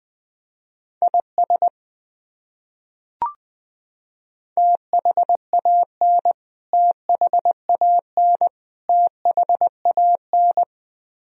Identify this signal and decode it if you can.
Morse code, heard as a keyed tone, about 21 wpm, 700 Hz. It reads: IS THAN THAN THAN
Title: Top 100 Words Repeated 20wpm